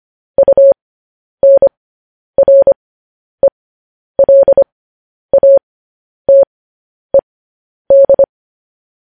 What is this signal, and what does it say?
Morse code, heard as a keyed tone, about 25 wpm, spaced as for 10 wpm, 550 Hz. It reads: UNRELATED
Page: news-headlines-in-morse-code-at-10-wpm